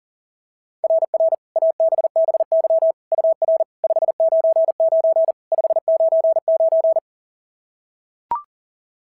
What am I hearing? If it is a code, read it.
Morse code, heard as a keyed tone, about 40 wpm, 660 Hz. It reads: RR ABBY UR 599 599
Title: SOTA QSO Practice 40wpm